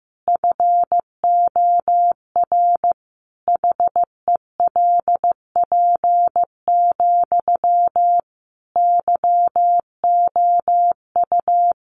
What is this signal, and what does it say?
Morse code, heard as a keyed tone, about 15 wpm, 700 Hz. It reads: FOR HELP, YOU
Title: Sentences from Top 300 Words 15wpm